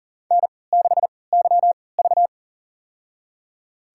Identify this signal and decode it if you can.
Morse code, heard as a keyed tone, about 42 wpm, spaced as for 23 wpm, 700 Hz. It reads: N6YV
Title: USA Call Signs - 4 Characters - 3x Character Spacing - 40wpm